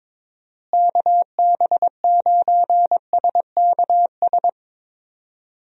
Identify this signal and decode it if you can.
Morse code, heard as a keyed tone, about 21 wpm, 700 Hz. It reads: KB9SKS
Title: USA Call Signs - 6 Characters - Repeated 2x - 22wpm